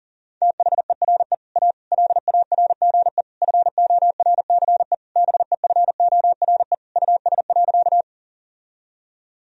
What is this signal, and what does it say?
Morse code, heard as a keyed tone, about 40 wpm, 700 Hz. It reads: THERE A LARGE FORCE BEFORE US.
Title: Sentences from Top 1000 Words - Review Part 3 - 40wpm